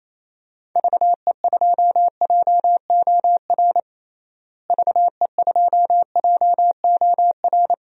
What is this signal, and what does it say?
Morse code, heard as a keyed tone, about 28 wpm, 700 Hz. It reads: VE2JOR VE2JOR
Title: DXCC-100 Call Signs - 6 Characters - Repeated 2x - 28wpm